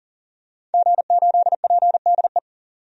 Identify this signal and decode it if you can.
Morse code, heard as a keyed tone, about 40 wpm, 700 Hz. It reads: G8PDE
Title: DXCC-100 Call Signs - 5 Characters - 40wpm